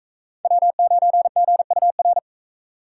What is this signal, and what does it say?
Morse code, heard as a keyed tone, about 44 wpm, 700 Hz. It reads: W9GUR